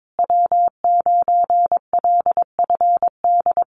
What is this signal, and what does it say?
Morse code, heard as a keyed tone, about 22 wpm, 700 Hz. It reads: W9LFD
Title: USA Call Signs Repeated 2x 22wpm